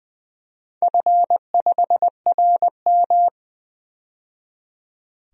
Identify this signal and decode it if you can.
Morse code, heard as a keyed tone, about 20 wpm, 700 Hz. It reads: F5RM
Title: DXCC-100 Call Signs - 4 Characters - 20wpm